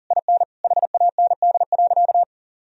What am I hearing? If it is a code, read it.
Morse code, heard as a keyed tone, about 40 wpm, 700 Hz. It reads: IN HAND.
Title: Sentences from Top 1000 Words - Review Part 4 - 40wpm